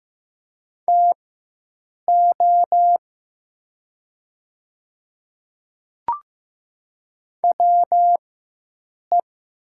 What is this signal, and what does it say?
Morse code, heard as a keyed tone, about 15 wpm, spaced as for 7 wpm, 700 Hz. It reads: TO WE
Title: 4x Character Spacing - Top 300 Words - 15wpm